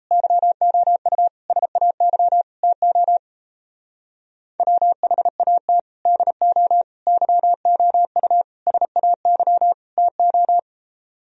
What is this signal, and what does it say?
Morse code, heard as a keyed tone, about 35 wpm, 700 Hz. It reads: YOU SAY TO WHAT DO YOU SAY TO
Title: Speed-Racing - Sets of 5 Words 28wpm